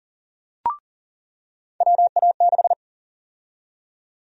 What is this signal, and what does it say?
Morse code, heard as a keyed tone, about 40 wpm, 700 Hz. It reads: WA6
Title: USA Call Sign Prefixes - 40wpm